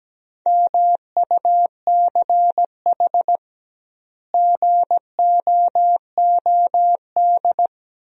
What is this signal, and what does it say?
Morse code, heard as a keyed tone, about 17 wpm, 700 Hz. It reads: MUCH GOOD
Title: 2x Word-Spacing - Sentences from Top 200 Words - 17wpm